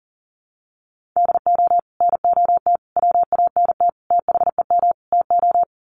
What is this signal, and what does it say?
Morse code, heard as a keyed tone, about 40 wpm, 700 Hz. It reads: DO NOT WANT THEM TO
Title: Sets of 5 Words Encore 40wpm